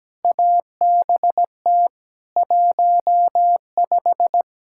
Morse code, heard as a keyed tone, about 17 wpm, 700 Hz. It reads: ABT 15
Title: Practice QSO - Exchange 3 - More Info - 17wpm